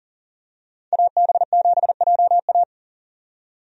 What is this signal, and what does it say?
Morse code, heard as a keyed tone, about 38 wpm, 700 Hz. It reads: AB7JA
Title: USA Call Signs - 5 Characters - 40wpm